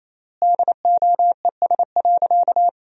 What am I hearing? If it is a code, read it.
Morse code, heard as a keyed tone, about 28 wpm, 700 Hz. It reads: DOES.
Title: 3x Word-Spacing - Sentences from Top 500 Words - 28wpm